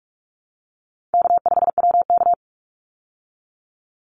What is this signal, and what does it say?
Morse code, heard as a keyed tone, about 44 wpm, 700 Hz. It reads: K5WK
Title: Easy Call Signs Repeated 3x - 45wpm